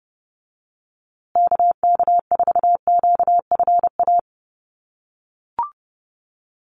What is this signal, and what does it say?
Morse code, heard as a keyed tone, about 30 wpm, 700 Hz. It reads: KK4QFA
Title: USA Call Signs Repeated 3x 30wpm